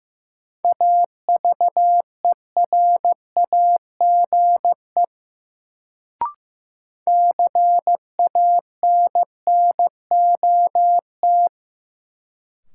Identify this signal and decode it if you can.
Morse code, heard as a keyed tone, about 15 wpm, 700 Hz. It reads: AVERAGE CANNOT
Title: Sentences from Top 1000 Words - Review Part 6 - New Word Drill - 15wpm